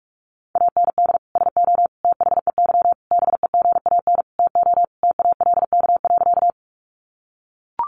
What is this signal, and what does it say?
Morse code, heard as a keyed tone, about 44 wpm, 700 Hz. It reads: AND SO THEY BEGAN TO TALK.
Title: Sentences from Top 400 Words 45wpm